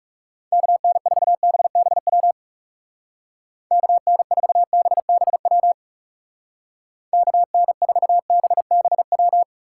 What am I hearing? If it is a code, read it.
Morse code, heard as a keyed tone, about 36 wpm, 700 Hz. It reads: KN4BBW KN4BBW KN4BBW
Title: Speed Racing - USA 2x3 Call Signs 30wpm